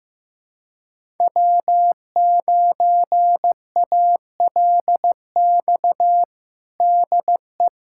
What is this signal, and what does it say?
Morse code, heard as a keyed tone, about 15 wpm, 700 Hz. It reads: W9ALX DE
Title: USA Callsign DE Callsign - 5 Characters - 15wpm